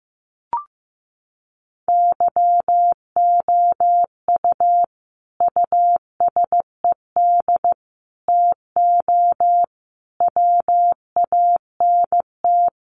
Morse code, heard as a keyed tone, about 15 wpm, 700 Hz. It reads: YOU USED TO WANT
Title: Sentences from Top 600 Words 15wpm